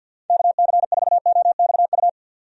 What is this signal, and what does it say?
Morse code, heard as a keyed tone, about 50 wpm, 700 Hz. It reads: KC4OXU
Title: USA Call Signs - 6 Characters - 50wpm